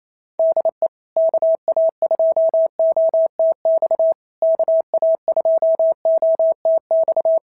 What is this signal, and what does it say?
Morse code, heard as a keyed tone, about 28 wpm, 650 Hz. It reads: DE KA2OTX KA2OTX
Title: SOTA QSO Practice 28wpm